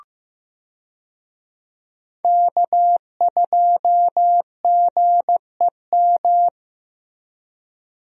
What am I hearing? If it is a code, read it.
Morse code, heard as a keyed tone, about 15 wpm, 700 Hz. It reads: K2GEM
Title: USA Call Signs - 5 Characters - Repeated 2x - 15wpm